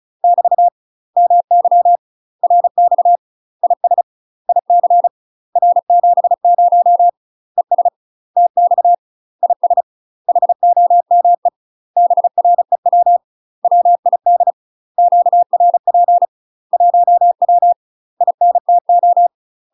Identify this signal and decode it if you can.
Morse code, heard as a keyed tone, about 36 wpm, 700 Hz. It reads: <BT> MY RX IS IC R70 ES TX IS HOME BREW WID QRP 1W INTO
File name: UK MASTER QSOs TRAINING / 35WPM / UK MASTER QSOs 0041